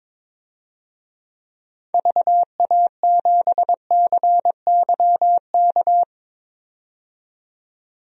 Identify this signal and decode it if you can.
Morse code, heard as a keyed tone, about 21 wpm, 700 Hz. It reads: VA7CYK
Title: DXCC-100 Call Signs - 6 Characters - 22wpm